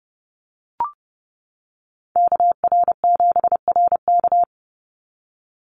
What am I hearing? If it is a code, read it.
Morse code, heard as a keyed tone, about 30 wpm, 700 Hz. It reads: KR7RK
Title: NA SOTA Chaser Callsigns 30wpm